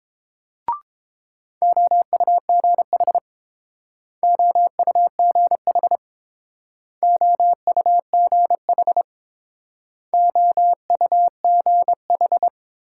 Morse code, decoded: OUGH OUGH OUGH OUGH

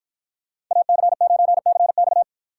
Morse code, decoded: AC9CX

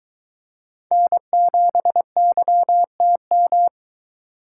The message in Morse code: N7YTM

